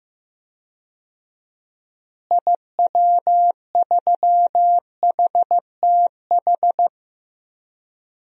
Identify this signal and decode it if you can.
Morse code, heard as a keyed tone, about 15 wpm, 700 Hz. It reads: IW3HTH
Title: DXCC-100 Call Signs - 6 Characters - Repeated 3x - 15wpm